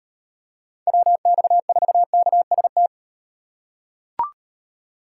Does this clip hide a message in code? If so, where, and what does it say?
Yes: Morse code from 0.9 to 2.9 s, heard as a keyed tone, about 38 wpm, 700 Hz. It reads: WX4KST